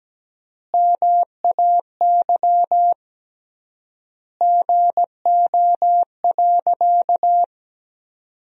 Morse code, heard as a keyed tone, about 17 wpm, 700 Hz. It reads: MAY GO.
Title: 3x Word-Spacing - Sentences from Top 100 Words - 17wpm